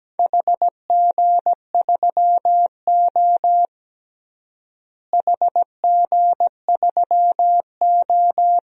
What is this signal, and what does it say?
Morse code, heard as a keyed tone, about 17 wpm, 700 Hz. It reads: HG3O HG3O
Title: DXCC-100 Call Signs - 4 Characters - Repeated 3x - 17wpm